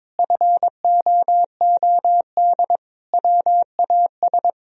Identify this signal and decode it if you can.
Morse code, heard as a keyed tone, about 21 wpm, 700 Hz. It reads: FOOD WAS
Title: Sentences from Top 1000 Words - Review Part 4 - 22wpm